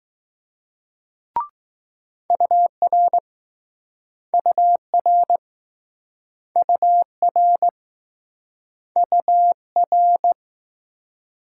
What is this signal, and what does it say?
Morse code, heard as a keyed tone, about 21 wpm, 700 Hz. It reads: UR UR UR UR